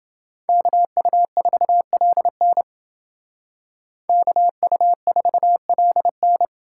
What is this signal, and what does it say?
Morse code, heard as a keyed tone, about 29 wpm, 700 Hz. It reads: KU4LN KU4LN